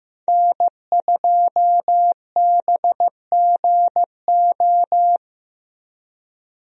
Morse code, heard as a keyed tone, about 15 wpm, 700 Hz. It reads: N2BGO
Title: USA Call Signs - 5 Characters - Repeated 3x - 15wpm